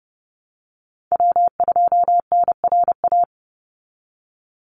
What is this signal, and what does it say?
Morse code, heard as a keyed tone, about 30 wpm, 700 Hz. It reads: W2NRA
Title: Call Signs Repeated 2x - 30wpm